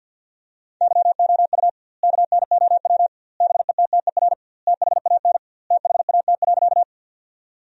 Morse code, heard as a keyed tone, about 50 wpm, 700 Hz. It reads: YOU KNOW BETTER THAN THAT.
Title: 2x Word-Spacing - Sentences from Top 400 Words - 50wpm